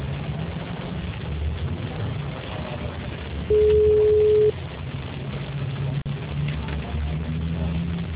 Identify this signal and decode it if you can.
Ringback tone (Europe, 425 Hz).
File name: out- -2014- - - .24275